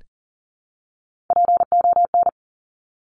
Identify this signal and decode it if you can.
Morse code, heard as a keyed tone, about 43 wpm, 700 Hz. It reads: PON
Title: Top Three Letter Combinations in Words 40wpm